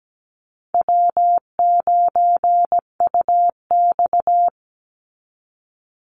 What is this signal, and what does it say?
Morse code, heard as a keyed tone, about 17 wpm, 700 Hz. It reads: W9UX
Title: NA SOTA Chaser Callsigns 17wpm